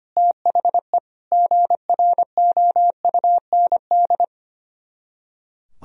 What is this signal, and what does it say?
Morse code, heard as a keyed tone, about 30 wpm, 700 Hz. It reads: THE GROUND
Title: Speed-Racing - Sets of 3 Words 25wpm